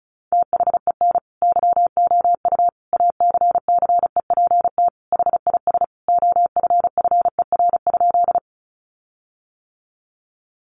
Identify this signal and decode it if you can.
Morse code, heard as a keyed tone, about 34 wpm, 700 Hz. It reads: THEN YOU ACCEPT HIS OFFER?
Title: Sentences from Top 1300 Words 35wpm